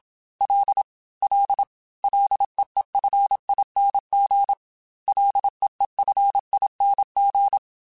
Morse code, heard as a keyed tone, about 27 wpm, 800 Hz. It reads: L L LEEFING LEEFING
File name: F and L / F and L mp3 / F and L Prefixes 3x2 Twice @25